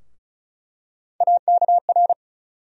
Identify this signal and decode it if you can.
Morse code, heard as a keyed tone, about 34 wpm, 700 Hz. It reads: AKR
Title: USA Call Sign Postfixes - 3 Letters 35wpm